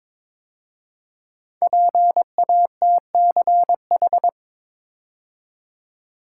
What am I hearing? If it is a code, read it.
Morse code, heard as a keyed tone, about 22 wpm, 700 Hz. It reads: PATCH
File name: 5 Letter Words - First 3 Letters Given - 22wpm